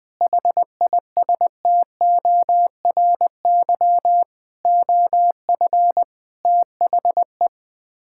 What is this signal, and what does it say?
Morse code, heard as a keyed tone, about 21 wpm, 690 Hz. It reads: HISTORY OF THE
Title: Build A Phrase - 4-Word Phrases - 20wpm